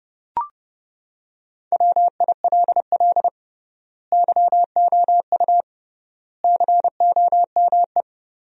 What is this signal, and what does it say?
Morse code, heard as a keyed tone, about 30 wpm, 700 Hz. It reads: WILL YOU COME
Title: 3x Word-Spacing - Sentences from Top 100 Words - 30wpm